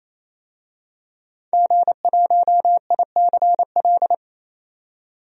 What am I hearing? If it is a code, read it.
Morse code, heard as a keyed tone, about 28 wpm, 700 Hz. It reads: G1ICL
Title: DXCC-100 Call Signs - 5 Characters - 28wpm